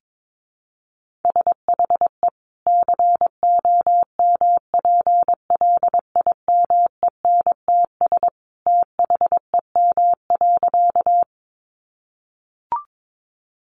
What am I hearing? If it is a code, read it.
Morse code, heard as a keyed tone, about 22 wpm, 700 Hz. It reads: SHE COMPLIMENTS THEM.
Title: Easy Phrases 22wpm